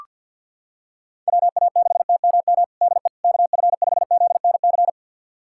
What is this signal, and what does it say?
Morse code, heard as a keyed tone, about 46 wpm, 700 Hz. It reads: WA6TMM DE KF5ZNC
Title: USA Callsign DE Callsign - 6 Characters - 50wpm